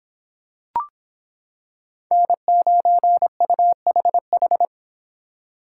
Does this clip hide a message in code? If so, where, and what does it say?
Yes: Morse code from 2.1 to 4.7 s, heard as a keyed tone, about 25 wpm, 700 Hz. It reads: N9UHH